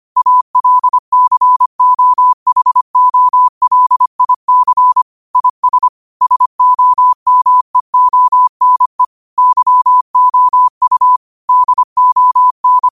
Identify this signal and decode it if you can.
Morse code, heard as a keyed tone, about 25 wpm, 1000 Hz. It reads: ALCOHOLIC IS SOMEONE YOU DON